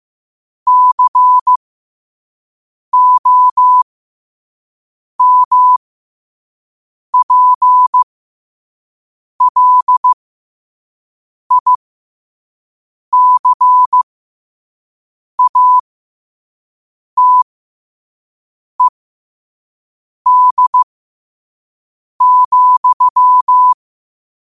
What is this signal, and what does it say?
Morse code, heard as a keyed tone, about 15 wpm, spaced as for 5 wpm, 1000 Hz. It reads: COMPLICATED,
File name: qotdfiles / 5wpm